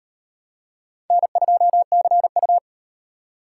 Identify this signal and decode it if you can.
Morse code, heard as a keyed tone, about 38 wpm, 700 Hz. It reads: N2CU